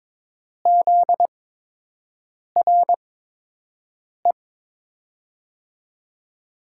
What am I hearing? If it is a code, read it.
Morse code, heard as a keyed tone, about 22 wpm, spaced as for 6 wpm, 700 Hz. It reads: ZRE
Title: USA Call Signs - 5 Characters - 8x Character Spacing - 22wpm